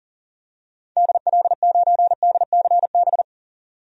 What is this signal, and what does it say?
Morse code, heard as a keyed tone, about 40 wpm, 700 Hz. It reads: DL9DCB
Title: DXCC-100 Call Signs - 6 Characters - 40wpm